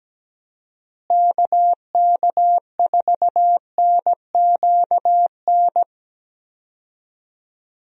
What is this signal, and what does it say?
Morse code, heard as a keyed tone, about 17 wpm, 700 Hz. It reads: KK4NQN
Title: USA Call Signs - 6 Characters - Repeated 3x - 17wpm